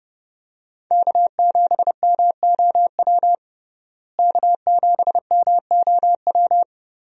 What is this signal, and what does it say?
Morse code, heard as a keyed tone, about 30 wpm, 700 Hz. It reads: K7MOW K7MOW
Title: USA Call Signs - 5 Characters - Repeated 2x - 30wpm